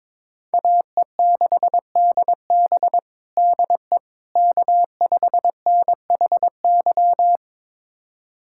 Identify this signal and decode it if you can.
Morse code, heard as a keyed tone, about 22 wpm, 700 Hz. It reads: AE6DB DE K5NHY
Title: USA Callsign DE Callsign - 5 Characters - 22wpm